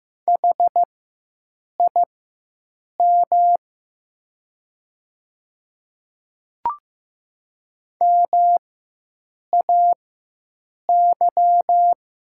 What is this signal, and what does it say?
Morse code, heard as a keyed tone, about 15 wpm, spaced as for 7 wpm, 700 Hz. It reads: HIM MAY
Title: 4x Character Spacing - Top 100 Words - 15wpm